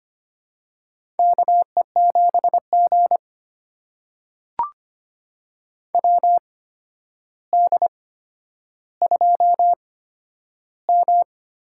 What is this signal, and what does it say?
Morse code, heard as a keyed tone, about 25 wpm, spaced as for 11 wpm, 700 Hz. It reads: KE7G WD2M
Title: USA Call Signs - 4 Characters - 8x Character Spacing - 25wpm